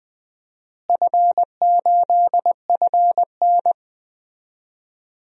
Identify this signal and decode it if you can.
Morse code, heard as a keyed tone, about 20 wpm, 700 Hz. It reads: F8FN
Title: DXCC-100 Call Signs - 4 Characters - Repeated 2x - 20wpm